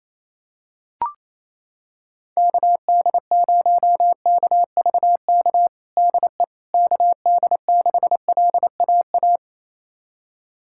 Morse code, heard as a keyed tone, about 28 wpm, 700 Hz. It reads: KD0KVK DE KD6LAA
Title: USA Callsign DE Callsign - 6 Characters - 28wpm